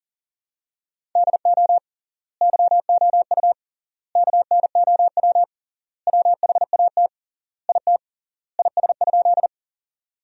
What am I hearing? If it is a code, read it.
Morse code, heard as a keyed tone, about 40 wpm, 700 Hz. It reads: DO YOU KNOW WHAT IT IS?
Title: 3x Word-Spacing - Sentences from Top 100 Words - 40wpm